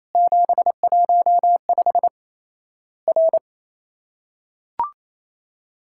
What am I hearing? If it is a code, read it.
Morse code, heard as a keyed tone, about 28 wpm, 700 Hz. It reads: 715 R
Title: Worked All Europe DX Contest - QTC Exchange - 28wpm